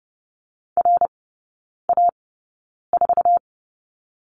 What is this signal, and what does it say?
Morse code, heard as a keyed tone, about 30 wpm, spaced as for 9 wpm, 700 Hz. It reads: RA4
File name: Lesson 19e - Letter w - Three Characters 30wpm